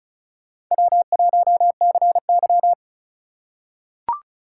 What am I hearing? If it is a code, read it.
Morse code, heard as a keyed tone, about 34 wpm, 700 Hz. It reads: W1CY
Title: USA Call Signs - 6x Character Spacing - 35wpm